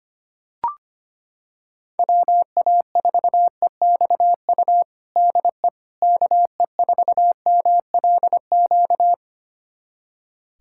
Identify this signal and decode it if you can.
Morse code, heard as a keyed tone, about 25 wpm, 700 Hz. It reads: WA4EXU DE KE4MLQ
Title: USA Callsign DE Callsign - 6 Characters - 25wpm